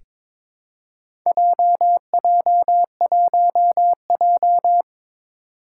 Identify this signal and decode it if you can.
Morse code, heard as a keyed tone, about 22 wpm, 700 Hz. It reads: JJ1J
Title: Four Challenging Letter-Number Combinations 22wpm